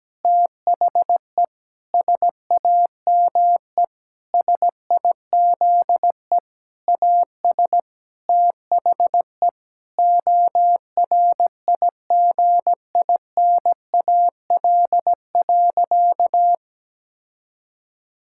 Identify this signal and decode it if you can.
Morse code, heard as a keyed tone, about 17 wpm, 700 Hz. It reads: THE SAME SIZE AS THE ORIGINAL.
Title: Sentences from Top 1000 Words - Review Part 6 - 17wpm